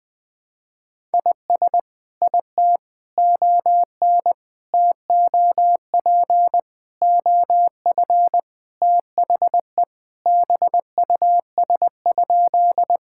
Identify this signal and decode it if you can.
Morse code, heard as a keyed tone, about 21 wpm, 700 Hz. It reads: IS IT ON TOP OF THE BUS?
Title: Sentences from Top 1000 Words - Review Part 6 - 20wpm